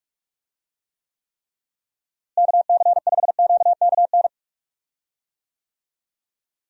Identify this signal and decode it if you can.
Morse code, heard as a keyed tone, about 46 wpm, 700 Hz. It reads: KC5QKN
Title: USA Call Signs - 6 Characters - Repeated 3x - 45wpm